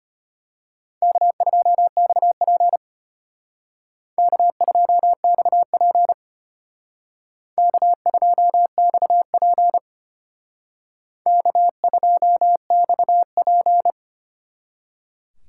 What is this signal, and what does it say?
Morse code, heard as a keyed tone, about 33 wpm, 700 Hz. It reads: K2XP K2XP K2XP K2XP